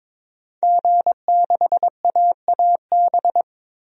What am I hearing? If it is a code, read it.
Morse code, heard as a keyed tone, about 22 wpm, 700 Hz. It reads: G6AAB
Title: DXCC-100 Call Signs - 5 Characters - Repeated 2x - 22wpm